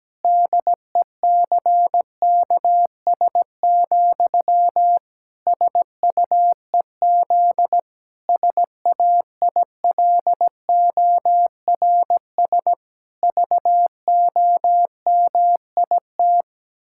Morse code, heard as a keyed tone, about 17 wpm, 700 Hz. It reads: DECKS, SUEZ SAILORS VOMIT
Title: Pangram Challenge v2 - 17wpm